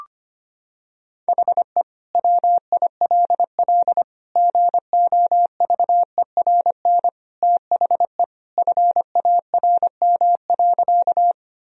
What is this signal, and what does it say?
Morse code, heard as a keyed tone, about 25 wpm, 700 Hz. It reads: HE WILL GOVERN THE FARM.
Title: Sentences from Top 1000 Words - Review Part 2 - 25wpm